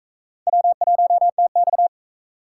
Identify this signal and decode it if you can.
Morse code, heard as a keyed tone, about 38 wpm, 700 Hz. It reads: W1TX